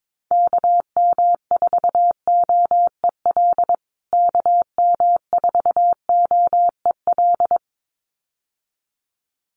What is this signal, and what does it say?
Morse code, heard as a keyed tone, about 22 wpm, 700 Hz. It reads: KM4OEL KM4OEL
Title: USA Call Signs Repeated 2x 22wpm